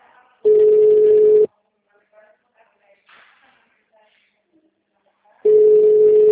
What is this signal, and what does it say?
Ringback tone (Europe, 425 Hz), with speech.
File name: out- -2017- - - .6181